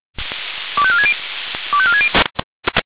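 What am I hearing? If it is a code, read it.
Five-tone selective call (ZVEI), heard as quick runs of five tones, tones of about 70 ms. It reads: 25460 25460
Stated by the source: ZVEI five-tone selective call 25459, received by SDR.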